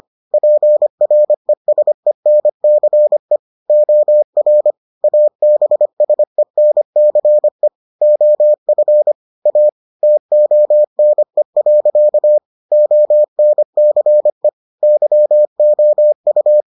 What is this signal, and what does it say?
Morse code, heard as a keyed tone, about 25 wpm, 600 Hz. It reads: PRESENCE OR ABSENCE OF A TONE. ONCE YOU